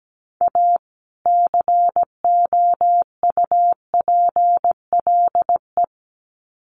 Morse code, heard as a keyed tone, about 17 wpm, 700 Hz. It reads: A COUPLE
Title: Sets of 2 Words 17wpm